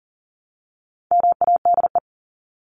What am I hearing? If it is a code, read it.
Morse code, heard as a keyed tone, about 40 wpm, 700 Hz. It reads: MADE
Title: Top 300 Words Repeated 40wpm